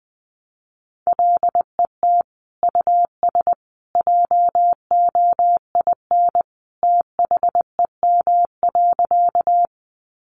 Morse code, heard as a keyed tone, about 20 wpm, 700 Hz. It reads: LET US JOIN THEM.
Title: Sentences from Top 1400 Words 20wpm